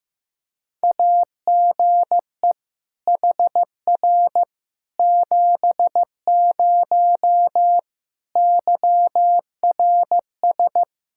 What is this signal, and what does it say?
Morse code, heard as a keyed tone, about 15 wpm, 700 Hz. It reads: AGE HR 70 YRS
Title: Practice QSO - Exchange 3 - More Info - 15wpm